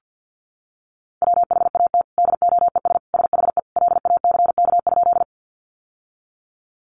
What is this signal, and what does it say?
Morse code, heard as a keyed tone, about 50 wpm, 700 Hz. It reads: WHAT DOES SHE LACK?
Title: Sentences from Top 1100 Words 50wpm